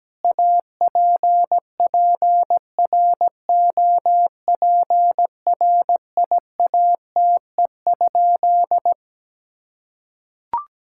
Morse code, heard as a keyed tone, about 17 wpm, 700 Hz. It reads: APPROPRIATE?
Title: Sentences from Top 1000 Words - Review Part 5 - 17wpm